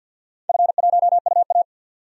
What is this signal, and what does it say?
Morse code, heard as a keyed tone, about 52 wpm, 700 Hz. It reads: F1UA
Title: DXCC-100 Call Signs - 4 Characters - 50wpm